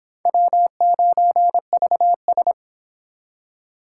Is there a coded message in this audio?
Yes: Morse code from 0.3 to 2.5 s, heard as a keyed tone, about 25 wpm, 700 Hz. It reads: W9VS